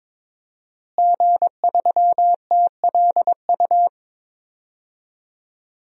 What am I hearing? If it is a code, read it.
Morse code, heard as a keyed tone, about 22 wpm, 700 Hz. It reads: G3TLU
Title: DXCC-100 Call Signs - 5 Characters - Repeated 3x - 22wpm